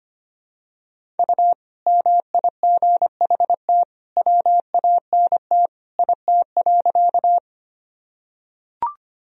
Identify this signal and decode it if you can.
Morse code, heard as a keyed tone, about 25 wpm, 700 Hz. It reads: U MIGHT WANT IT.
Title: Sentences from Top 300 Words and Abbreviations 25wpm